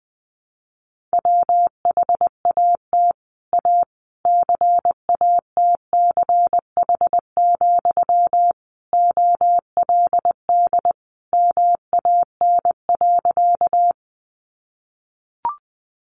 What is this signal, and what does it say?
Morse code, heard as a keyed tone, about 20 wpm, 700 Hz. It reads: WHAT A CATCH, OLD MAN.
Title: Sentences from Top 1500 Words 20wpm